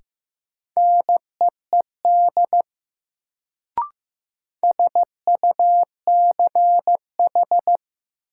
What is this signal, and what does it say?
Morse code, heard as a keyed tone, about 15 wpm, 700 Hz. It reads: NEED SUCH
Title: Sentences from Top 200 Words - New Word Drill - 15wpm